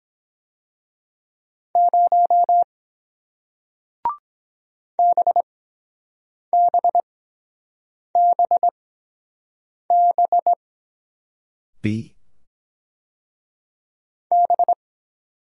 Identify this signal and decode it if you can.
Morse code, heard as a keyed tone, about 23 wpm, spaced as for 7 wpm, 700 Hz. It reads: 0 BBBB B